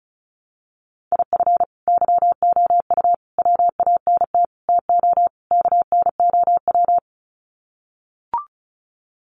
Morse code, heard as a keyed tone, about 34 wpm, 700 Hz. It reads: IF YOU WANT TO KNOW
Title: Sets of 5 Words Encore2 35wpm